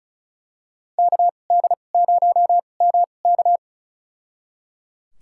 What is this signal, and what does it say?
Morse code, heard as a keyed tone, about 36 wpm, spaced as for 25 wpm, 700 Hz. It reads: KD0MK
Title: USA Call Signs - 2x Character Spacing - 35wpm